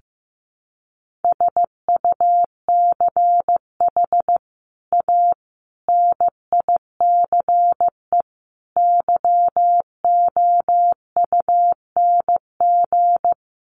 Morse code, heard as a keyed tone, about 15 wpm, 700 Hz. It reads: SUCH A NICE YOUNG
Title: Sentences from Top 1100 Words 15wpm